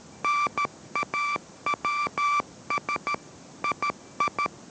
Morse code, heard as a keyed tone, about 13 wpm, 1170 Hz. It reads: NAWS II